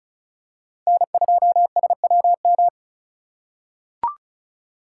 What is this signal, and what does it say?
Morse code, heard as a keyed tone, about 36 wpm, 700 Hz. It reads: N2SWM
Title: USA Call Signs - 5 Characters - 2x Character Spacing - 35wpm